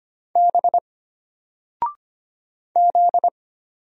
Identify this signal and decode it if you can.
Morse code, heard as a keyed tone, about 25 wpm, 700 Hz. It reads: B Z
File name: Single-Letter-Number-25wpm